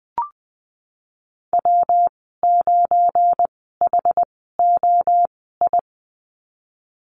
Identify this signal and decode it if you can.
Morse code, heard as a keyed tone, about 19 wpm, spaced as for 14 wpm, 700 Hz. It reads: W9HOI
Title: 179 Lesson 29g - Slash - Call Signs 20wpm